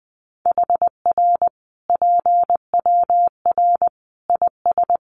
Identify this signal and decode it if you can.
Morse code, heard as a keyed tone, about 20 wpm, 700 Hz. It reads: HR PWR IS
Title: Common QSO Phrases 20wpm